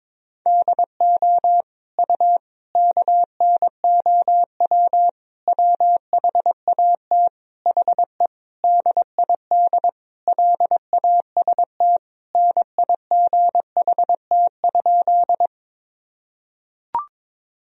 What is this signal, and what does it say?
Morse code, heard as a keyed tone, about 22 wpm, 700 Hz. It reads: DO U KNOW WHAT HE DID LAST NIGHT?
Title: Sentences from Top 300 Words and Abbreviations 22wpm